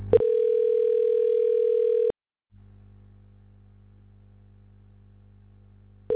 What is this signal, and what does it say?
Ringback tone (North America), 440 + 480 Hz.